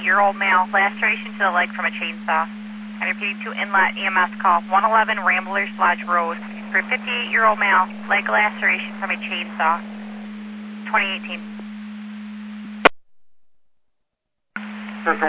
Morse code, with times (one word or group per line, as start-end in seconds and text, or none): none